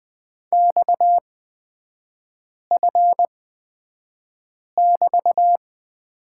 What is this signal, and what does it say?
Morse code, heard as a keyed tone, about 20 wpm, spaced as for 5 wpm, 700 Hz. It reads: XF<BT>
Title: 243 Lesson 39e - Letter z - Three Characters 20wpm